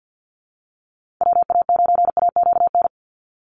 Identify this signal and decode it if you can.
Morse code, heard as a keyed tone, about 50 wpm, 700 Hz. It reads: WA9AQN
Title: Call Signs Repeated 2x - 50wpm